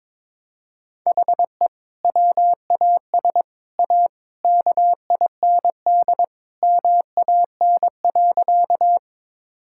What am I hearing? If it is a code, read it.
Morse code, heard as a keyed tone, about 22 wpm, 700 Hz. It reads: HE WAS A KIND MAN.
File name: practice / 22wpm / Sentences with 3-Word Phrases - 22wpm